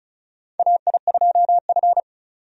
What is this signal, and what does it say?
Morse code, heard as a keyed tone, about 34 wpm, 700 Hz. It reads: AI2F
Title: USA Call Signs - 4 Characters - 35wpm